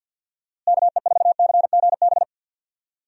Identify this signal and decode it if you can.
Morse code, heard as a keyed tone, about 50 wpm, 700 Hz. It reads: KE4CGB
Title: USA Call Signs - 6 Characters - Repeated 3x - 50wpm